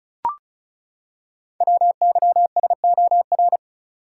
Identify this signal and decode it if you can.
Morse code, heard as a keyed tone, about 36 wpm, 690 Hz. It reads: WYSOR
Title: Code Groups - 5 Letters - 35wpm